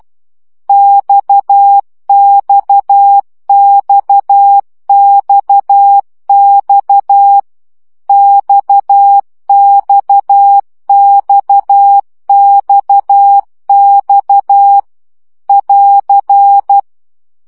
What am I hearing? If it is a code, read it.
Morse code, heard as a keyed tone, about 12 wpm, 800 Hz. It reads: XXXXX XXXXX <AR>
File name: CW gyakorlas x 10 baud